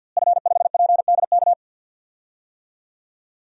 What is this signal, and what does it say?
Morse code, heard as a keyed tone, about 52 wpm, 700 Hz. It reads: F5PDK
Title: DXCC-100 Call Signs - 5 Characters - Repeated 3x - 50wpm